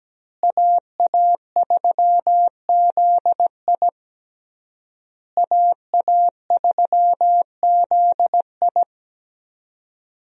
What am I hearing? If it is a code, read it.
Morse code, heard as a keyed tone, about 17 wpm, 700 Hz. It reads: AA3ZI AA3ZI
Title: USA Call Signs - 5 Characters - Repeated 3x - 17wpm